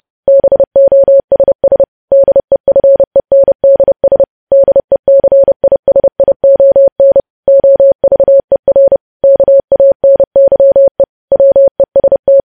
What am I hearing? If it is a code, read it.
Morse code, heard as a keyed tone, about 30 wpm, 550 Hz. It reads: BOSS DEFENDS DECISION OVER KANYE WEST